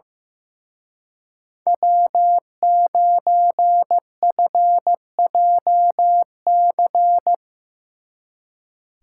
Morse code, heard as a keyed tone, about 16 wpm, 700 Hz. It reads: W9FJC
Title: Speed Racing - USA 1x3 Call Signs 15wpm